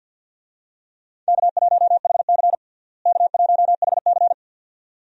Morse code, heard as a keyed tone, about 51 wpm, 700 Hz. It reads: K1HC K1HC
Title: USA Call Signs - 4 Characters - Repeated 2x - 50wpm